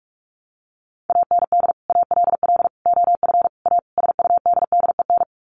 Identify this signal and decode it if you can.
Morse code, heard as a keyed tone, about 44 wpm, 700 Hz. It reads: AND ALL OF A SUDDEN
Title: Sets of 5 Words 45wpm